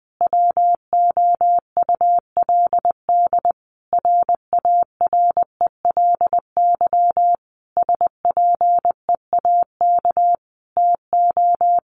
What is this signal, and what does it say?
Morse code, heard as a keyed tone, about 20 wpm, 700 Hz. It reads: WOULD RARELY SPEAK TO
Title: Sentences from Top 1400 Words 20wpm